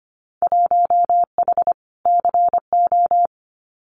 Morse code, heard as a keyed tone, about 25 wpm, 700 Hz. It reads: 1H CO
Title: Winter Field Day Exchanges 25wpm